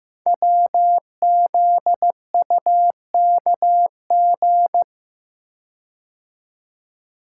Morse code, heard as a keyed tone, about 15 wpm, 700 Hz. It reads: WZUKG
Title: Code Groups - 5 Letters - 15wpm